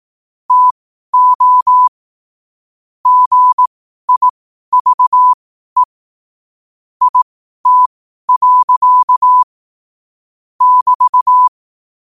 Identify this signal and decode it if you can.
Morse code, heard as a keyed tone, about 18 wpm, spaced as for 12 wpm, 1000 Hz. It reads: TO GIVE IT. <BT>
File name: qotdfiles / 12wpm